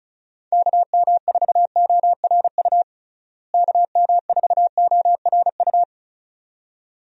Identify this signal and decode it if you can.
Morse code, heard as a keyed tone, about 36 wpm, 700 Hz. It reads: KM4ORU KM4ORU
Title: USA Call Signs - 6 Characters - Repeated 2x - 35wpm